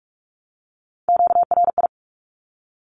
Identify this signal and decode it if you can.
Morse code, heard as a keyed tone, about 44 wpm, 700 Hz. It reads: QRI
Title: Q-Codes Repeated 3x 45wpm